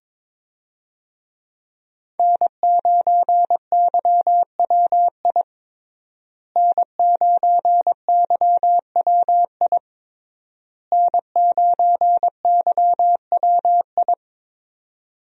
Morse code, heard as a keyed tone, about 21 wpm, 700 Hz. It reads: N9YWI N9YWI N9YWI
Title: USA Call Signs - 5 Characters - Repeated 3x - 22wpm